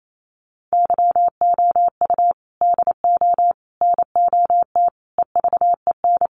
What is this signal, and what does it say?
Morse code, heard as a keyed tone, about 28 wpm, 700 Hz. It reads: YOU DO NOT EVEN
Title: Sets of 5 Words Encore2 28wpm